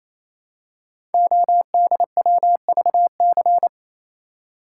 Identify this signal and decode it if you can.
Morse code, heard as a keyed tone, about 27 wpm, 700 Hz. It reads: ODWVC
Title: Code Groups - 5 Letters - 28wpm